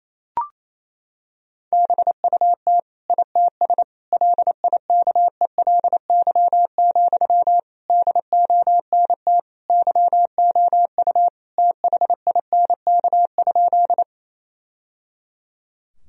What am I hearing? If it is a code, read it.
Morse code, heard as a keyed tone, about 28 wpm, 700 Hz. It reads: BUT ITS LIKELY, DONT YOU THINK?
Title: Sentences from Top 1000 Words - Review Part 4 - 28wpm